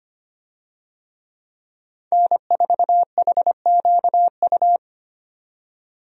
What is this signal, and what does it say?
Morse code, heard as a keyed tone, about 25 wpm, 690 Hz. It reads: N4HQU
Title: USA Call Signs - 5 Characters - 25wpm